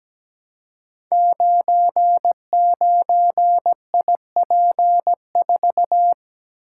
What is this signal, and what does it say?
Morse code, heard as a keyed tone, about 17 wpm, 700 Hz. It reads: 99IP4
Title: Code Groups - 5 Letter:Numbers - 17wpm